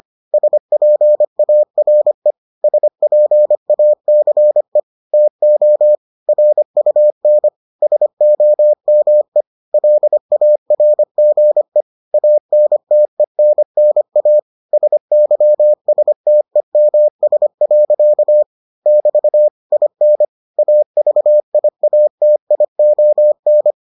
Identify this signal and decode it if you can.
Morse code, heard as a keyed tone, about 25 wpm, 600 Hz. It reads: SPARE SPACE TO RUN SOME LARGE ANTENNA SYSTEMS. <BT> IN AVIATION